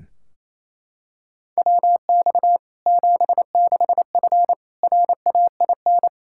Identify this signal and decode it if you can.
Morse code, heard as a keyed tone, about 28 wpm, 700 Hz. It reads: WX 76F RAIN
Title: Practice QSO - Exchange 2 - Info - 28wpm